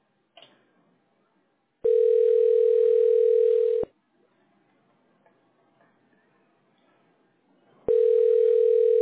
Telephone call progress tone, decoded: ringback tone